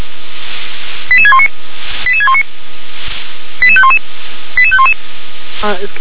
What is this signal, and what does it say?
Five-tone selective call (ZVEI), heard as quick runs of five tones, tones of about 70 ms, with speech. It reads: 88619 88619 88511 88511